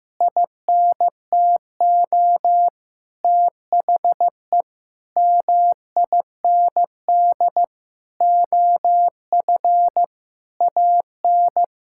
Morse code, heard as a keyed tone, about 15 wpm, 700 Hz. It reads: INTO THE MIND OF AN
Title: Sentences from Top 1600 Words 15wpm